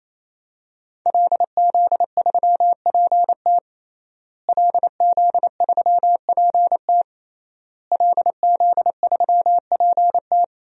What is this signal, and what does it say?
Morse code, heard as a keyed tone, about 28 wpm, 700 Hz. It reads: LZ3PT LZ3PT LZ3PT
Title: DXCC-100 Call Signs - 5 Characters - Repeated 3x - 28wpm